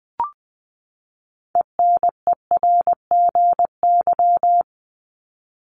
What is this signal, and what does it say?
Morse code, heard as a keyed tone, about 20 wpm, 700 Hz. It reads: ENERGY
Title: Top 751-1000 Words 20wpm